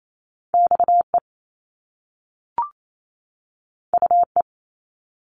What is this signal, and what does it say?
Morse code, heard as a keyed tone, about 28 wpm, 700 Hz. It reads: X E U E
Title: Top Two Letter Combinations in Words 28wpm